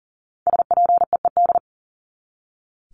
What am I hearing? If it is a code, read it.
Morse code, heard as a keyed tone, about 40 wpm, 700 Hz. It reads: SPEED
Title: Top 751-1000 Words 40wpm